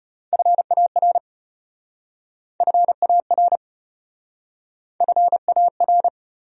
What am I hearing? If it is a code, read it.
Morse code, heard as a keyed tone, about 33 wpm, 700 Hz. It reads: FAR FAR FAR